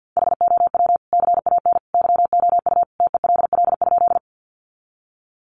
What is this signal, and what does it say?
Morse code, heard as a keyed tone, about 50 wpm, 700 Hz. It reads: HOW CAN YOU TELL?
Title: Sentences from Top 200 Words 50wpm